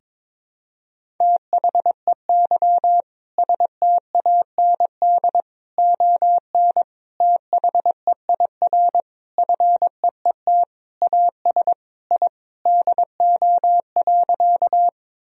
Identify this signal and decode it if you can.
Morse code, heard as a keyed tone, about 22 wpm, 700 Hz. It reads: THEY STAND ON THEIR FEET AS I DO.
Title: Sentences from Top 1000 Words - Review Part 1 - 22wpm